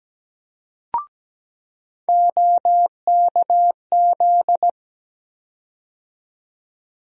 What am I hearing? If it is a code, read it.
Morse code, heard as a keyed tone, about 17 wpm, 700 Hz. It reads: OKZ
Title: USA Call Sign Postfixes - 3 Letters 17wpm